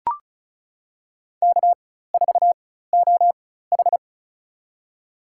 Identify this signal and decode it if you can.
Morse code, heard as a keyed tone, about 36 wpm, spaced as for 16 wpm, 700 Hz. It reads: K4OH
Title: USA Call Signs - 4 Characters - 4x Character Spacing - 35wpm